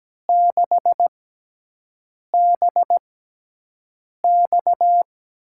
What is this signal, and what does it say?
Morse code, heard as a keyed tone, about 17 wpm, spaced as for 6 wpm, 700 Hz. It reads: 6BX
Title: USA Call Signs - 6x Character Spacing - 17wpm